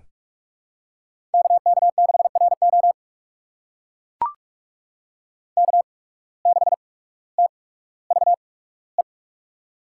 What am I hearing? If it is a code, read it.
Morse code, heard as a keyed tone, about 44 wpm, spaced as for 19 wpm, 700 Hz. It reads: KK6RO K6TVE
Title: USA Call Signs - 5 Characters - 8x Character Spacing - 45wpm